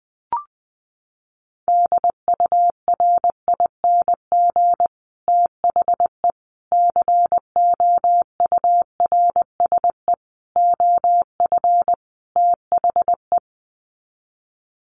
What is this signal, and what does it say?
Morse code, heard as a keyed tone, about 19 wpm, 700 Hz. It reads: DURING THE COURSE OF THE
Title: Sets of 5 Words Encore 20wpm